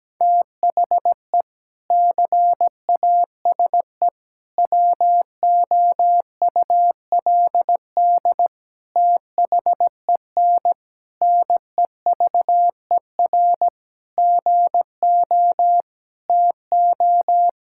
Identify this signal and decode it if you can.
Morse code, heard as a keyed tone, about 17 wpm, 700 Hz. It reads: THE CASE WOULD THEN NEVER GO TO
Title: Sentences from Top 1600 Words 17wpm